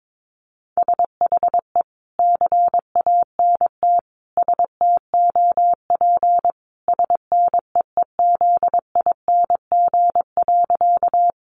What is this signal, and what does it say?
Morse code, heard as a keyed tone, about 22 wpm, 700 Hz. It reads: SHE CANT STOP SNEEZING.
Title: Easy Phrases 22wpm